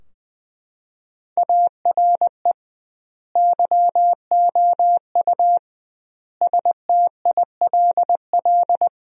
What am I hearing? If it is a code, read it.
Morse code, heard as a keyed tone, about 20 wpm, 700 Hz. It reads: ARE YOU STILL
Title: 2x Word-Spacing - Sentences from Top 400 Words - 20wpm